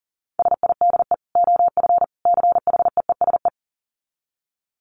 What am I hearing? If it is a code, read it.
Morse code, heard as a keyed tone, about 40 wpm, 700 Hz. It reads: SIDE OF CHEESE
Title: Dit Challenge 40wpm